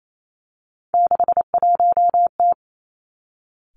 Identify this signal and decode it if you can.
Morse code, heard as a keyed tone, about 28 wpm, 700 Hz. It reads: 61T
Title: Three Challenging Letter-Number Combinations 28wpm